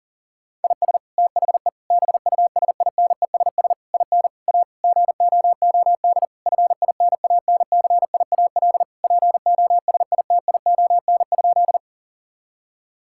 Morse code, heard as a keyed tone, about 40 wpm, 700 Hz. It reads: IS THE BUSINESS IN A GOOD FINANCIAL POSITION?
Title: Sentences from Top 1000 Words - Review Part 5 - 40wpm